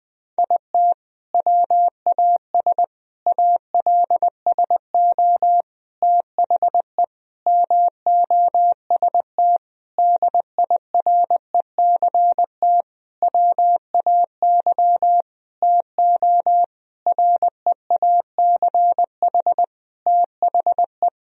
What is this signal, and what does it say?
Morse code, heard as a keyed tone, about 20 wpm, 700 Hz. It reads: IT WAS ALSO THE MOST DIRECT WAY TO REACH THE
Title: Sentences from Top 1000 Words - Review Part 2 - 20wpm